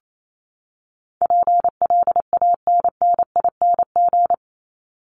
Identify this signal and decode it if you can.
Morse code, heard as a keyed tone, about 28 wpm, 700 Hz. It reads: PLANNING
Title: ING Suffix - Top 100 Words with Base Words 28wpm